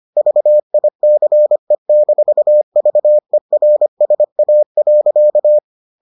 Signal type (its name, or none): Morse code